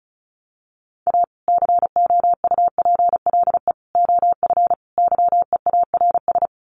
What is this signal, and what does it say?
Morse code, heard as a keyed tone, about 34 wpm, 700 Hz. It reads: A COUPLE OF YEARS
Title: Sets of 4 Words 35wpm